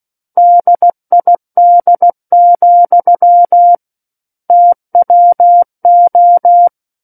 Morse code, heard as a keyed tone, about 16 wpm, 700 Hz. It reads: DID, TWO
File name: Book One / 07-Chapter 7 - 16 WPM